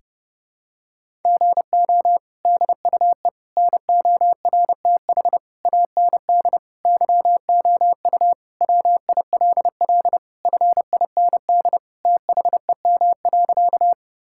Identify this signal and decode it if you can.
Morse code, heard as a keyed tone, about 30 wpm, 700 Hz. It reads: GO DUE NORTH AND YOU WILL FIND THEM.
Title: Sentences from Top 1000 Words - Review Part 6 - 30wpm